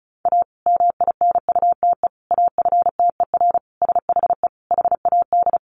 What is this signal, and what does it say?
Morse code, heard as a keyed tone, about 34 wpm, 700 Hz. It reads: A MINUTE AFTER SHE HAD
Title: Sentences from Top 1400 Words 35wpm